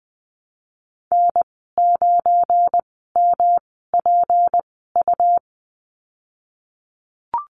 Morse code, heard as a keyed tone, about 19 wpm, spaced as for 14 wpm, 700 Hz. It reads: N9MPU
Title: 173 Lesson 28g - Number 9 - Call Signs 20wpm